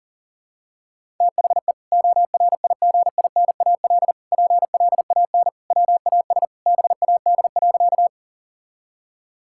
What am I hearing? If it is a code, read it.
Morse code, heard as a keyed tone, about 40 wpm, 700 Hz. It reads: THE ORIGINAL PLAN WAS BAD.
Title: Sentences from Top 1000 Words - Review Part 6 - 40wpm